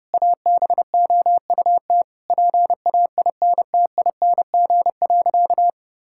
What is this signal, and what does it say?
Morse code, heard as a keyed tone, about 28 wpm, 700 Hz. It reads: ABOUT PAINTING.
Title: Sentences from Top 1000 Words - Review Part 6 - 30wpm